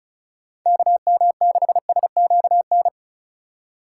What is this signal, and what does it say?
Morse code, heard as a keyed tone, about 36 wpm, 700 Hz. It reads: KM6SQN
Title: USA Call Signs - 6 Characters - 4x Character Spacing - 35wpm